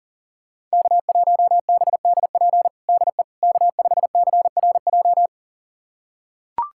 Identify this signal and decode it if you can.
Morse code, heard as a keyed tone, about 40 wpm, 700 Hz. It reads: K1BDP DE K5CRJ
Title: USA Callsign DE Callsign - 5 Characters - 40wpm